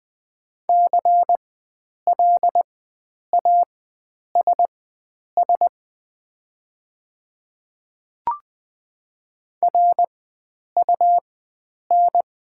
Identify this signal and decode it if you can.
Morse code, heard as a keyed tone, about 20 wpm, spaced as for 9 wpm, 700 Hz. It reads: CLASS RUN
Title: 4x Character Spacing - Top 500 Words - 20wpm